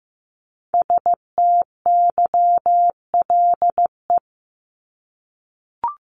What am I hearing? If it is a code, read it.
Morse code, heard as a keyed tone, about 15 wpm, 700 Hz. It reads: STYLE
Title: Top 751-1000 Words 15wpm